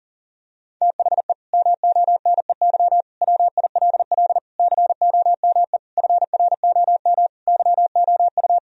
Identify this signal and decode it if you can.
Morse code, heard as a keyed tone, about 39 wpm, 700 Hz. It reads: THE MONEY WILL COME FROM YOU
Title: Sentences from Top 1000 Words - Review Part 4 - 40wpm